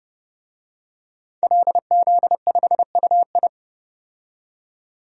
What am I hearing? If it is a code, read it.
Morse code, heard as a keyed tone, about 32 wpm, 700 Hz. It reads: LZ5UI
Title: DXCC-100 Call Signs - 5 Characters - 30wpm